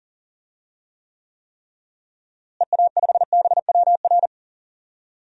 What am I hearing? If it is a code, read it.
Morse code, heard as a keyed tone, about 40 wpm, 700 Hz. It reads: EA5BWR
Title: DXCC-100 Call Signs - 6 Characters - Repeated 3x - 40wpm